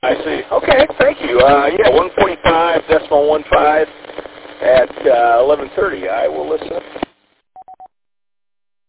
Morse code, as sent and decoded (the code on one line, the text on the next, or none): none